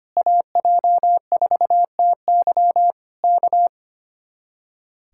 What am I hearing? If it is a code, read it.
Morse code, heard as a keyed tone, about 24 wpm, 700 Hz. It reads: AJ4TY K
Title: Speed-Racing Common QSO Phrases Encore - 25wpm